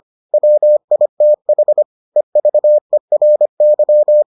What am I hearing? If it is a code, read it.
Morse code, heard as a keyed tone, about 25 wpm, 600 Hz. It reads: WITH EVERY